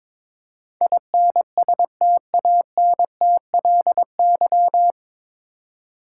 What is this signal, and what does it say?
Morse code, heard as a keyed tone, about 22 wpm, 700 Hz. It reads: INSTANTLY
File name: 9 Letter Words - First 3 Letters Given - 22wpm